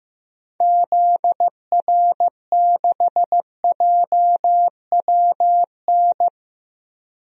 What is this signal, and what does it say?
Morse code, heard as a keyed tone, about 15 wpm, 700 Hz. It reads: ZR6JWN
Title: DXCC-100 Call Signs - 6 Characters - Repeated 2x - 15wpm